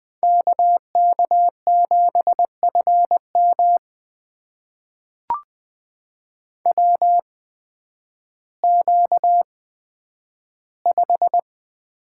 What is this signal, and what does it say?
Morse code, heard as a keyed tone, about 20 wpm, spaced as for 11 wpm, 700 Hz. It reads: KK7FM WQ5
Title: USA Call Signs - 8x Character Spacing - 20wpm